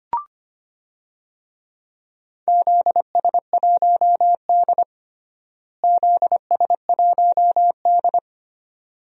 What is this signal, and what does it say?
Morse code, heard as a keyed tone, about 25 wpm, 700 Hz. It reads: ZS1D ZS1D
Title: DXCC-100 Call Signs - 4 Characters - Repeated 2x - 25wpm